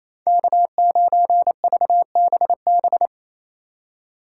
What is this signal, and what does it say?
Morse code, heard as a keyed tone, about 33 wpm, 700 Hz. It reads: K9VBB
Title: Speed Racing - USA 1x3 Call Signs 28wpm